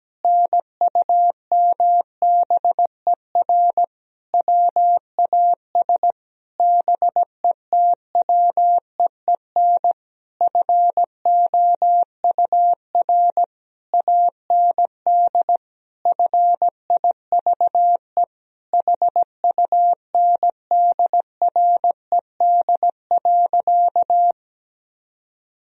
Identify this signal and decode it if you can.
Morse code, heard as a keyed tone, about 17 wpm, 700 Hz. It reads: NUMBER WAS BETWEEN FOUR AND FIVE HUNDRED.
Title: Sentences from Top 1000 Words - Review Part 2 - 17wpm